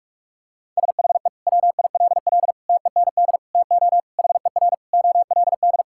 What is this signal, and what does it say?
Morse code, heard as a keyed tone, about 46 wpm, 700 Hz. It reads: SHE WILL TEND TO HER OLD
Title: Sentences from Top 1000 Words - Review Part 5 - 45wpm